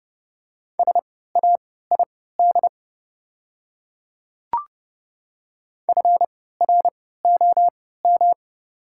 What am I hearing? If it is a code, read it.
Morse code, heard as a keyed tone, about 30 wpm, spaced as for 17 wpm, 690 Hz. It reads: SAID FROM
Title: 3x Character Spacing - Top 100 Words - 30wpm